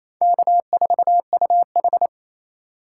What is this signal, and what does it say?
Morse code, heard as a keyed tone, about 28 wpm, 700 Hz. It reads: K4UH
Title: USA Call Signs - 2x Character Spacing - 28wpm